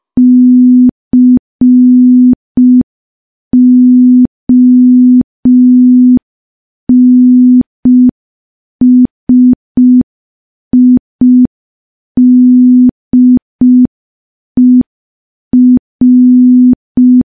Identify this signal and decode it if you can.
Morse code, heard as a keyed tone, about 5 wpm, 250 Hz. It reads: CONSIDER
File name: morse / morse-training-05WPM